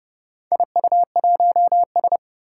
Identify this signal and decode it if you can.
Morse code, heard as a keyed tone, about 30 wpm, 690 Hz. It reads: IU1S
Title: Four Challenging Letter-Number Combinations 30wpm